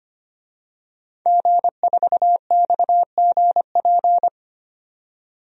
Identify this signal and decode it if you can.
Morse code, heard as a keyed tone, about 25 wpm, 700 Hz. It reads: G4XGP
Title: DXCC-100 Call Signs - 5 Characters - 25wpm